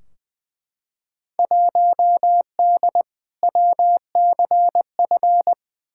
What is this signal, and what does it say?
Morse code, heard as a keyed tone, about 20 wpm, 700 Hz. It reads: 1D WCF
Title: Field Day Exchanges - 20wpm